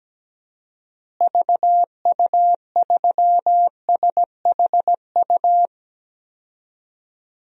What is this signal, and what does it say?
Morse code, heard as a keyed tone, about 17 wpm, 700 Hz. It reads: VU3SHU
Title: DXCC-100 Call Signs - 6 Characters - Repeated 2x - 17wpm